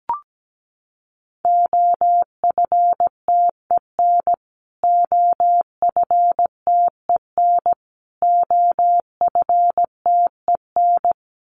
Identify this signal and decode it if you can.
Morse code, heard as a keyed tone, about 17 wpm, 700 Hz. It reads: OFTEN OFTEN OFTEN
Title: Top 300 Words Repeated 17wpm